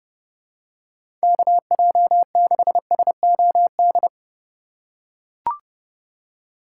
Morse code, heard as a keyed tone, about 32 wpm, 700 Hz. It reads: KJ6SOD
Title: USA Call Signs - 6 Characters - 2x Character Spacing - 30wpm